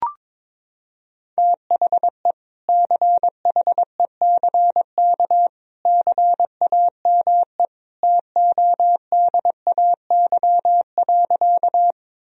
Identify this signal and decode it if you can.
Morse code, heard as a keyed tone, about 22 wpm, 700 Hz. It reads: THE CHECK CAME TODAY.
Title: Sentences from Top 1000 Words - Review Part 3 - 22wpm